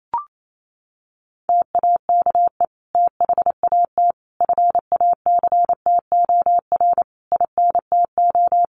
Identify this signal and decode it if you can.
Morse code, heard as a keyed tone, about 28 wpm, 700 Hz. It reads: TAKE THAT FACTOR INTO
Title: Sentences from Top 1300 Words 28wpm